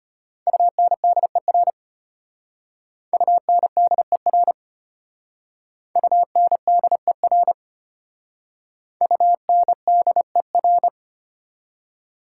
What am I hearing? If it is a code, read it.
Morse code, heard as a keyed tone, about 32 wpm, 700 Hz. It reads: UNDER UNDER UNDER UNDER